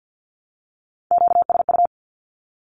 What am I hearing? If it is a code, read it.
Morse code, heard as a keyed tone, about 50 wpm, 700 Hz. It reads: QSU
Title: Q-Codes Repeated 3x 50wpm